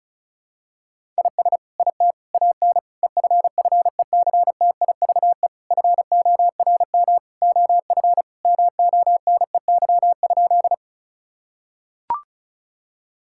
Morse code, heard as a keyed tone, about 36 wpm, 700 Hz. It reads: IS IT AN EFFECTIVE FORM OF MONEY?
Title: Sentences from Top 1000 Words - Review Part 6 - 35wpm